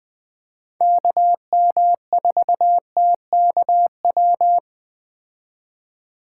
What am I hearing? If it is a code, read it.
Morse code, heard as a keyed tone, about 20 wpm, 700 Hz. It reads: KM4TKW
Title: Speed Racing - USA 2x3 Call Signs 17wpm